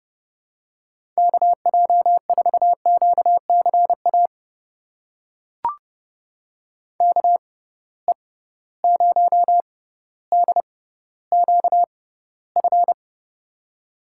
Morse code, heard as a keyed tone, about 30 wpm, spaced as for 15 wpm, 700 Hz. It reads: KJ4QCA KE0DQF
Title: USA Call Signs - 6 Characters - 6x Character Spacing - 30wpm